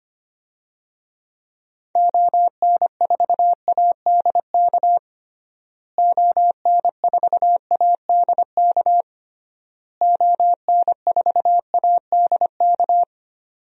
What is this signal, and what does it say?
Morse code, heard as a keyed tone, about 25 wpm, 700 Hz. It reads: ON4ADK ON4ADK ON4ADK
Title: DXCC-100 Call Signs - 6 Characters - Repeated 3x - 25wpm